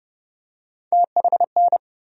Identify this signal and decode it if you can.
Morse code, heard as a keyed tone, about 29 wpm, 700 Hz. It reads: THN
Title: USA Call Sign Postfixes - 3 Letters 30wpm